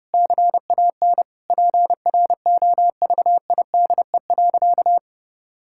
Morse code, heard as a keyed tone, about 32 wpm, 700 Hz. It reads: CAN PROVIDE.
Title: Sentences from Top 1000 Words - Review Part 6 - 30wpm